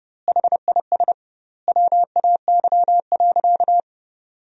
Morse code, heard as a keyed tone, about 30 wpm, 700 Hz. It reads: HIS WAY.
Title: 2x Word-Spacing - Sentences from Top 300 Words - 30wpm